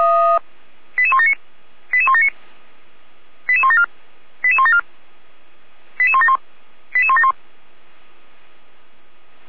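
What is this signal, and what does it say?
Five-tone selective call (ZVEI), heard as quick runs of five tones, tones of about 70 ms. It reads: 88179 88179 88174 88174 88171 88171
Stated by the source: ZVEI five-tone fire brigade selective call received on an SDR by Feuerwehr Orlamünde.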